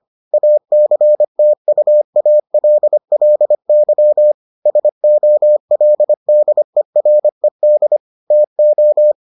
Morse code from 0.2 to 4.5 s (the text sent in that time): ACTUALLY